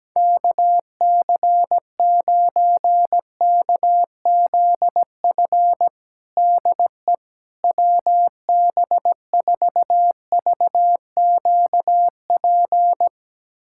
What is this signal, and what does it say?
Morse code, heard as a keyed tone, about 17 wpm, 700 Hz. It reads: KC9KZF DE WB4VQP
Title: USA Callsign DE Callsign - 6 Characters - 17wpm